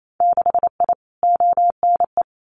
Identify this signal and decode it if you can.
Morse code, heard as a keyed tone, about 28 wpm, 700 Hz. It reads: 6I ONE
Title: Winter Field Day Exchanges Encore 28wpm